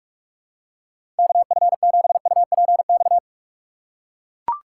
Morse code, heard as a keyed tone, about 44 wpm, 700 Hz. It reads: KF7UPX
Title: USA Call Signs - 6 Characters - 4x Character Spacing - 45wpm